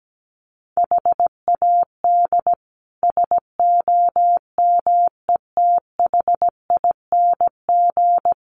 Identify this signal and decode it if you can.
Morse code, heard as a keyed tone, about 17 wpm, 700 Hz. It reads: HAD SOMETHING
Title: Sets of 5 Words 17wpm